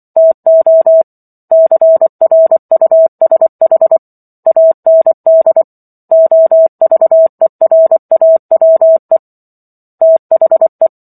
Morse code, heard as a keyed tone, about 24 wpm, 650 Hz. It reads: TO CRUSH AND OVERAWE THE